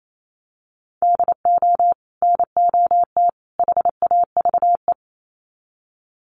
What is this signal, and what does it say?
Morse code, heard as a keyed tone, about 28 wpm, 700 Hz. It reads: DO NOT HAVE
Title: Sets of 3 Words 28wpm